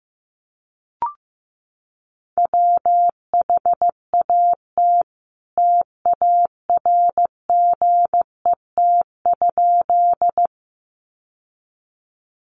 Morse code, heard as a keyed tone, about 15 wpm, 700 Hz. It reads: WHAT TARGET?
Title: Sentences from Top 1200 Words 15wpm